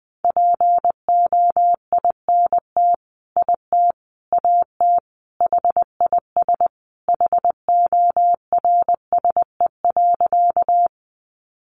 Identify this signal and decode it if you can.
Morse code, heard as a keyed tone, about 20 wpm, 700 Hz. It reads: POINT IT AT HIS HORSE.
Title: Sentences from Top 300 Words 20wpm